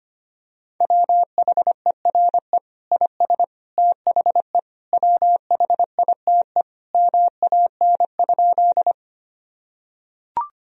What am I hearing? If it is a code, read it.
Morse code, heard as a keyed tone, about 25 wpm, 700 Hz. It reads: WHERE IS THE WHITE MAN?
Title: Sentences from Top 1000 Words - Review Part 1 - 25wpm